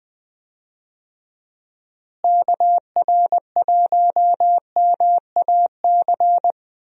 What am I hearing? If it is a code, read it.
Morse code, heard as a keyed tone, about 20 wpm, 700 Hz. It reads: KR1MAC
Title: USA Call Signs - 6 Characters - Repeated 3x - 20wpm